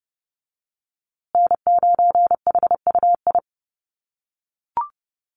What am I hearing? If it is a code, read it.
Morse code, heard as a keyed tone, about 30 wpm, 700 Hz. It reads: N9HUI
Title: USA Call Signs Repeated 3x 30wpm